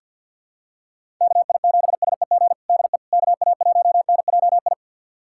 Morse code, heard as a keyed tone, about 51 wpm, 700 Hz. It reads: KI7SEG DE KA1NJI
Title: USA Callsign DE Callsign - 6 Characters - 50wpm